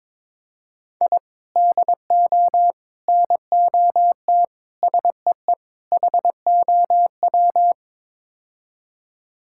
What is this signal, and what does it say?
Morse code, heard as a keyed tone, about 22 wpm, 700 Hz. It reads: I DO NOT SEE HOW
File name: practice / 22wpm / Build A Phrase - 5-Word Phrases - 22wpm